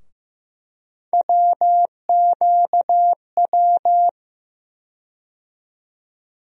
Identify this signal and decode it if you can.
Morse code, heard as a keyed tone, about 15 wpm, 700 Hz. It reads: WQW
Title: USA Call Sign Postfixes - 3 Letters 15wpm